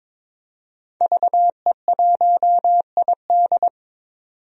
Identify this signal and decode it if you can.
Morse code, heard as a keyed tone, about 22 wpm, 700 Hz. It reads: VE1ID
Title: DXCC-100 Call Signs - 5 Characters - 22wpm